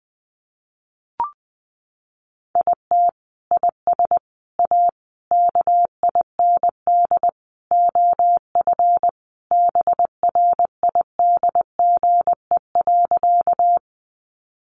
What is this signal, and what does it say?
Morse code, heard as a keyed tone, about 20 wpm, 700 Hz. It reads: IT IS A KIND OF BRIDGE.
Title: Sentences from Top 1200 Words 20wpm